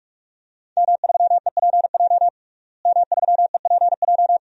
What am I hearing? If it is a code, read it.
Morse code, heard as a keyed tone, about 46 wpm, 700 Hz. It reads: M3EPJ M3EPJ
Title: DXCC-100 Call Signs - 5 Characters - Repeated 2x - 45wpm